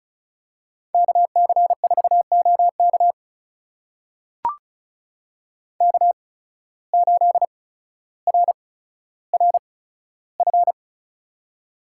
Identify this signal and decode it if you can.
Morse code, heard as a keyed tone, about 36 wpm, spaced as for 15 wpm, 700 Hz. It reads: KC4OK K8RRF
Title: USA Call Signs - 5 Characters - 8x Character Spacing - 35wpm